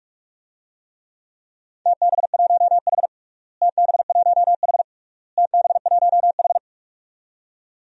Morse code, heard as a keyed tone, about 46 wpm, 700 Hz. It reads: TB1H TB1H TB1H
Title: DXCC-100 Call Signs - 4 Characters - Repeated 3x - 45wpm